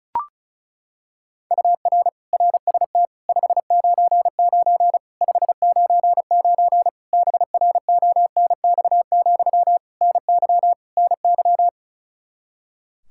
Morse code, heard as a keyed tone, about 36 wpm, 700 Hz. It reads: UR RST 599 599 BRONX, NY NY
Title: Practice QSO - Exchange 1 - Beginning - 35wpm